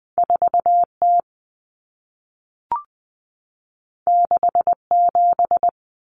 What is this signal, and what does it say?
Morse code, heard as a keyed tone, about 20 wpm, 700 Hz. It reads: 4T 67
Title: Common Cut Numbers - 2-Digit 20wpm